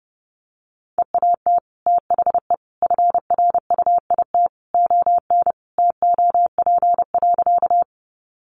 Morse code, heard as a keyed tone, about 30 wpm, 700 Hz. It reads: EAT THE FRUIT ON TOP.
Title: Sentences from Top 1300 Words 30wpm